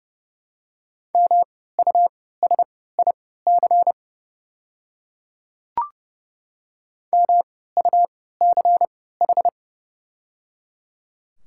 Morse code, heard as a keyed tone, about 30 wpm, spaced as for 17 wpm, 700 Hz. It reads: MUSIC MUCH
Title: 3x Character Spacing - Top 300 Words - 30wpm